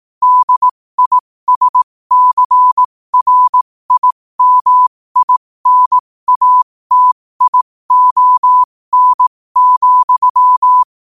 Morse code, heard as a keyed tone, about 18 wpm, spaced as for 15 wpm, 1000 Hz. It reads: DISCRIMINATION,
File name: qotdfiles / 15wpm